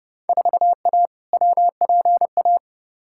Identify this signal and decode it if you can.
Morse code, heard as a keyed tone, about 30 wpm, 700 Hz. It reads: 4A WPA
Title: Field Day Exchanges Encore - 30wpm